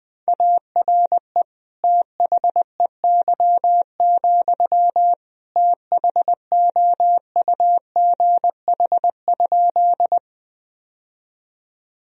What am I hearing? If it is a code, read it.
Morse code, heard as a keyed tone, about 20 wpm, 700 Hz. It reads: ARE THEY, THOUGH?
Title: Sentences from Top 1000 Words - Review Part 2 - 20wpm